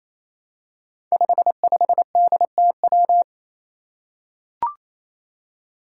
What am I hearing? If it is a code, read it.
Morse code, heard as a keyed tone, about 29 wpm, 700 Hz. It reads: 55DTW
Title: Five Challenging Letter-Number Combinations 28wpm